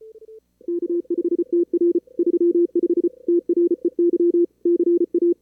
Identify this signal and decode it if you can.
Morse code, heard as a keyed tone, about 34 wpm, 340 Hz. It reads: K5TR 35 TREY CA
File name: CWSprint / Feb2022 / k5tr-feb-na-sprint-cw-k5ot -2300